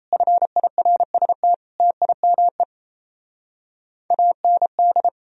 Morse code, heard as a keyed tone, about 37 wpm, 700 Hz. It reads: FIRST TIME AND